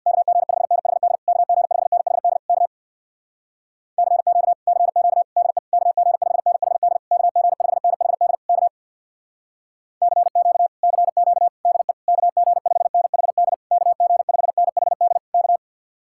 Morse code, heard as a keyed tone, about 64 wpm, 700 Hz. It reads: KG5NHD KG5NHD K CQ CQ DE KG5NHD KG5NHD K CQ CQ DE KG5NHD KG5NHD K